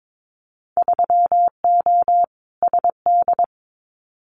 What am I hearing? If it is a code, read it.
Morse code, heard as a keyed tone, about 22 wpm, 700 Hz. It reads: 3O SD
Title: Winter Field Day Exchanges 22wpm